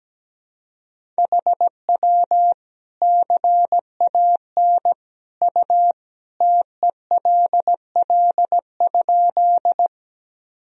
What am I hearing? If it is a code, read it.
Morse code, heard as a keyed tone, about 17 wpm, 700 Hz. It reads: HW CAN U TELL?
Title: Sentences from Top 200 Words and Abbreviations 17wpm